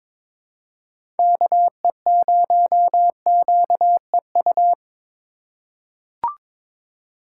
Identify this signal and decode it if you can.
Morse code, heard as a keyed tone, about 21 wpm, 700 Hz. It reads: KE0QEU
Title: USA Call Signs - 6 Characters - 4x Character Spacing - 22wpm